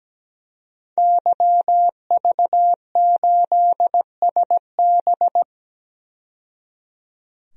Morse code, heard as a keyed tone, about 17 wpm, 700 Hz. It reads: YV8SB
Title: DXCC-100 Call Signs - 5 Characters - 17wpm